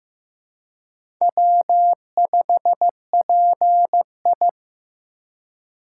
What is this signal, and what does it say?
Morse code, heard as a keyed tone, about 15 wpm, 700 Hz. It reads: W5PI
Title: USA Call Signs - 4 Characters - 2x Character Spacing - 15wpm